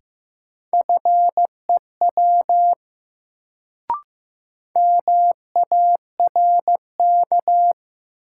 Morse code, heard as a keyed tone, about 15 wpm, 700 Hz. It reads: FEW MARK
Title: Sentences from Top 300 Words - New Word Drill - 15wpm